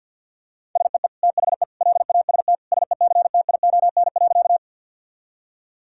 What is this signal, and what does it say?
Morse code, heard as a keyed tone, about 51 wpm, 700 Hz. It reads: SEE THE LAST SECTION.
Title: Sentences from Top 1000 Words - Review Part 5 - 50wpm